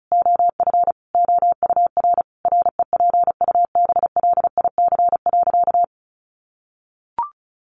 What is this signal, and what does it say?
Morse code, heard as a keyed tone, about 34 wpm, 700 Hz. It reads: OF OUR REPUBLIC.
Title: Sentences from Top 1500 Words 35wpm